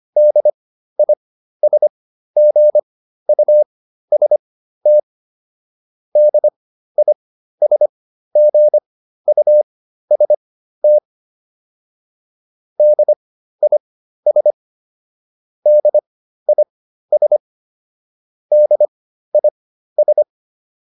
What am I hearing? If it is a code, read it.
Morse code, heard as a keyed tone, about 25 wpm, spaced as for 13 wpm, 600 Hz. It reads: DISGUST DISGUST DIS DIS DIS
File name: DIS7 13